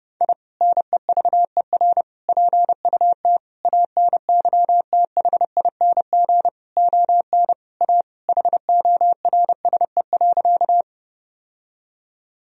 Morse code, heard as a keyed tone, about 32 wpm, 700 Hz. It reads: I NEVER PUT ANYTHING ON A HORSE.
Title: Sentences from Top 1000 Words - Review Part 1 - 30wpm